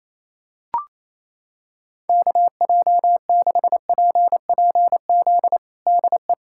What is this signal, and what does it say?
Morse code, heard as a keyed tone, about 28 wpm, 700 Hz. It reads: KJ6PPZ DE
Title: USA Callsign DE Callsign 28wpm